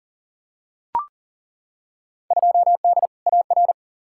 Morse code, heard as a keyed tone, about 40 wpm, 700 Hz. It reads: 2D AR
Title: Field Day Exchanges Encore - 40wpm